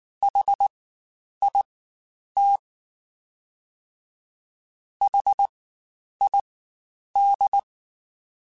Morse code, heard as a keyed tone, about 19 wpm, spaced as for 9 wpm, 800 Hz. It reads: HIT HID
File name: Spaced Alphas / Spaced Alphas mp3s / Hs sp @18